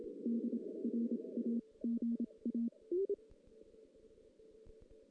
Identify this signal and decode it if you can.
Morse code, heard as a keyed tone, about 27 wpm, 230 Hz. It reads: D R A G A
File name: 2016 JAN CW NAQP / 2016 Jan naqp k5ot - 0-4